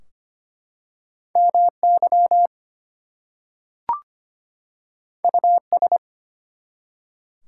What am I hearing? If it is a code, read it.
Morse code, heard as a keyed tone, about 25 wpm, 700 Hz. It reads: MY US
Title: Two Letter Words - 25wpm